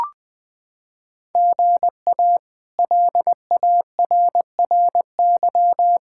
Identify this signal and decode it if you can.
Morse code, heard as a keyed tone, about 19 wpm, 700 Hz. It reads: GA LARRY
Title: K1USN SST QSO Practice - 20wpm